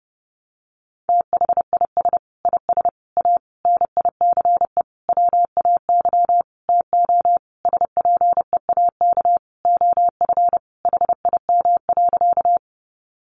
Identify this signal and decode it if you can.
Morse code, heard as a keyed tone, about 30 wpm, 700 Hz. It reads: THIS IS A NICE WAY TO SPEAK OF HIM.
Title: Sentences from Top 1100 Words 30wpm